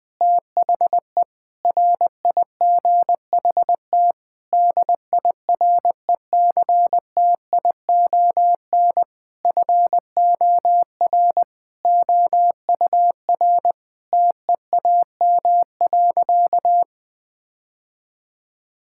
Morse code, heard as a keyed tone, about 21 wpm, 700 Hz. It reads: THE RIGHT DIRECTION FOR OUR TEAM.
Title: Sentences with 8-Word Phrases - 20wpm